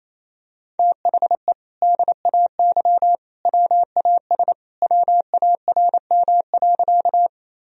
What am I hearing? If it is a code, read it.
Morse code, heard as a keyed tone, about 28 wpm, 700 Hz. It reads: THE DAY WAS WARM.
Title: Sentences from Top 1000 Words - Review Part 2 - 28wpm